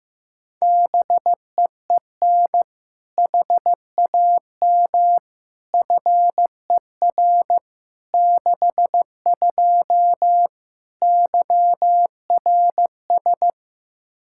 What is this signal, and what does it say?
Morse code, heard as a keyed tone, about 15 wpm, 700 Hz. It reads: BEEN HAM FER 62 YRS
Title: K1USN SST QSO Practice 15wpm